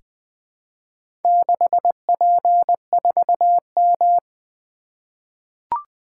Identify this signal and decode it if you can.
Morse code, heard as a keyed tone, about 20 wpm, 700 Hz. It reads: 6P4M
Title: Four Letter-Number Combinations - 20wpm